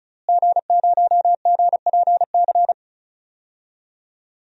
Morse code, heard as a keyed tone, about 36 wpm, 700 Hz. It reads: G0GPC
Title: DXCC-100 Call Signs - 5 Characters - Repeated 3x - 35wpm